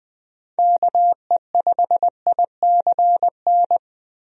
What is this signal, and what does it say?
Morse code, heard as a keyed tone, about 20 wpm, 700 Hz. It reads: KE5ICN
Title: USA Call Signs - 6 Characters - Repeated 2x - 20wpm